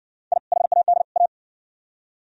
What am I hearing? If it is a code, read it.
Morse code, heard as a keyed tone, about 54 wpm, 700 Hz. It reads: I HAD A